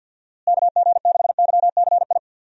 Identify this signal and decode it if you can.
Morse code, heard as a keyed tone, about 50 wpm, 700 Hz. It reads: KG6YCI
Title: USA Call Signs - 6 Characters - Repeated 3x - 50wpm